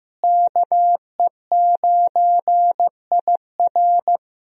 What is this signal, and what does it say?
Morse code, heard as a keyed tone, about 16 wpm, 700 Hz. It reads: KE9IR
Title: USA Call Signs - 5 Characters - Repeated 2x - 15wpm